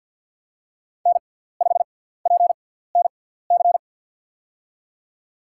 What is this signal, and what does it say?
Morse code, heard as a keyed tone, about 53 wpm, spaced as for 17 wpm, 700 Hz. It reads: N5PNC
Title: USA Call Signs - 5 Characters - 6x Character Spacing - 50wpm